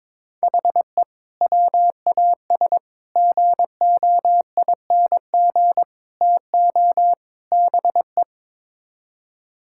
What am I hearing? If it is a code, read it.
Morse code, heard as a keyed tone, about 22 wpm, 700 Hz. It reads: HE WAS GOING TO BE
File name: practice / 22wpm / Build A Phrase - 5-Word Phrases - 22wpm